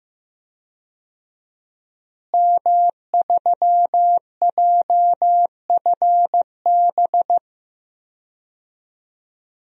Morse code, heard as a keyed tone, about 16 wpm, 700 Hz. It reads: M3JFB
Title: DXCC-100 Call Signs - 5 Characters - Repeated 2x - 15wpm